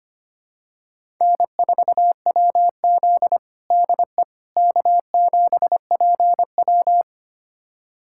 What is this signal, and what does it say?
Morse code, heard as a keyed tone, about 25 wpm, 700 Hz. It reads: N4WZ DE K7PW
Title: USA Callsign DE Callsign - 4 Characters - 25wpm